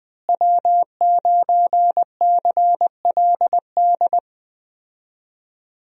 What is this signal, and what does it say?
Morse code, heard as a keyed tone, about 20 wpm, 700 Hz. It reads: W9CLD
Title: USA Call Signs - 5 Characters - Repeated 2x - 20wpm